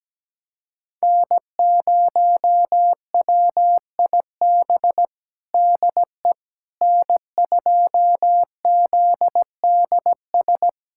Morse code, heard as a keyed tone, about 17 wpm, 700 Hz. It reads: N0WIB DE N2ZDS
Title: USA Callsign DE Callsign - 5 Characters - 17wpm